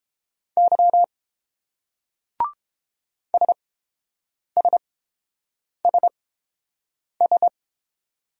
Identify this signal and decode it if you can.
Morse code, heard as a keyed tone, about 30 wpm, spaced as for 7 wpm, 700 Hz. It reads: Y SSSS